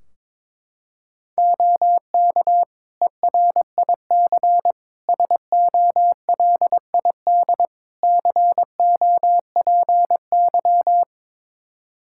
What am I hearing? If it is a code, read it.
Morse code, heard as a keyed tone, about 22 wpm, 700 Hz. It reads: OK ERIC SOLID COPY
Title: Practice QSO - Exchange 3 - More Info - 22wpm